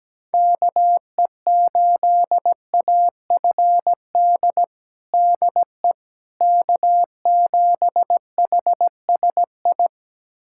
Morse code, heard as a keyed tone, about 17 wpm, 700 Hz. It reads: KE8AFD DE K7HSI
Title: USA Callsign DE Callsign 17wpm